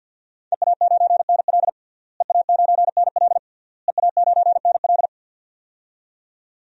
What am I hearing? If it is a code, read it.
Morse code, heard as a keyed tone, about 52 wpm, 700 Hz. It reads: EA9NL EA9NL EA9NL
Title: DXCC-100 Call Signs - 5 Characters - Repeated 3x - 50wpm